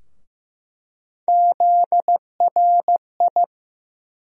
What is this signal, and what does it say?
Morse code, heard as a keyed tone, about 15 wpm, 700 Hz. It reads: ZRI
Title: USA Call Sign Postfixes - 3 Letters 15wpm